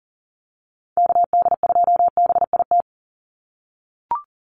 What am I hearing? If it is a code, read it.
Morse code, heard as a keyed tone, about 40 wpm, 700 Hz. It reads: KD2BIT
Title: USA Call Signs 40wpm